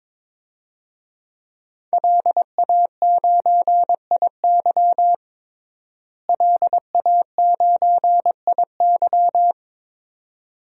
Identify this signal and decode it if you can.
Morse code, heard as a keyed tone, about 22 wpm, 700 Hz. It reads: LA9IY LA9IY
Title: DXCC-100 Call Signs - 5 Characters - Repeated 2x - 22wpm